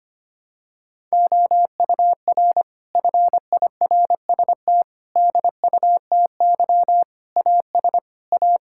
Morse code, heard as a keyed tone, about 25 wpm, 700 Hz. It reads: OUR FIRST DUTY AS A
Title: Sentences from Top 1600 Words 25wpm